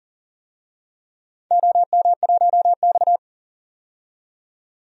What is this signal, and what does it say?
Morse code, heard as a keyed tone, about 42 wpm, 700 Hz. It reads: OM1X
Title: DXCC-100 Call Signs - 4 Characters - 40wpm